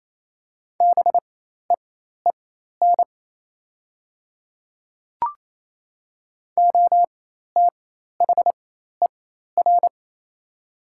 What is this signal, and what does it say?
Morse code, heard as a keyed tone, about 28 wpm, spaced as for 13 wpm, 700 Hz. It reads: BEEN OTHER
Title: 4x Character Spacing - Top 100 Words - 28wpm